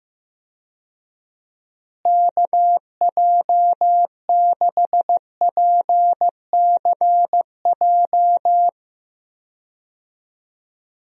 Morse code, heard as a keyed tone, about 15 wpm, 700 Hz. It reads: KJ6PCJ
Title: USA Call Signs - 6 Characters - Repeated 2x - 15wpm